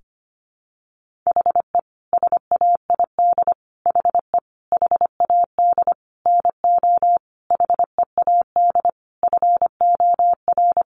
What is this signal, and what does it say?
Morse code, heard as a keyed tone, about 25 wpm, 700 Hz. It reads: HE SAID HE HAD NO HEAD FOR
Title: Sentences from Top 1500 Words 25wpm